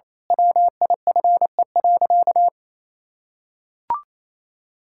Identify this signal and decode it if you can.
Morse code, heard as a keyed tone, about 28 wpm, 700 Hz. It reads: WIFE.
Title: Pangram Challenge v2 - 28wpm